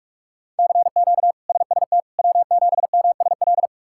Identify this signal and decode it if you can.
Morse code, heard as a keyed tone, about 42 wpm, 700 Hz. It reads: CQ SST W7MSL
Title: K1USN SST QSO Practice - 45wpm